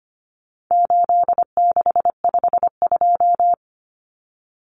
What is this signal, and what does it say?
Morse code, heard as a keyed tone, about 25 wpm, 700 Hz. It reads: 8652
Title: Common Cut Numbers - 4-Digit 25wpm